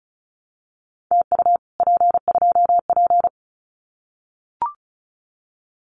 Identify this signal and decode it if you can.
Morse code, heard as a keyed tone, about 36 wpm, 700 Hz. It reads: TU P2P
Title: Common QSO Phrases Encore 35wpm